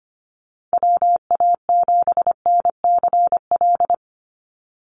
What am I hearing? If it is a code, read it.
Morse code, heard as a keyed tone, about 25 wpm, 700 Hz. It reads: WA7NCL
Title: NA SOTA Activator Callsigns 25wpm